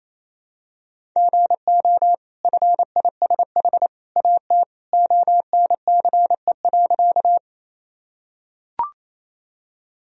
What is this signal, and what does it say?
Morse code, heard as a keyed tone, about 27 wpm, 700 Hz. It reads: GO FISH AT ONCE.
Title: Sentences from Top 1000 Words - Review Part 1 - 28wpm